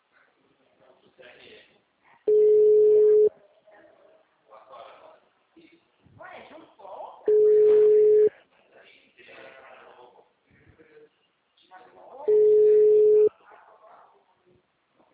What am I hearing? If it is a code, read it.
Ringback tone (Europe, 425 Hz), with speech.
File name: out- -2014- - - .52821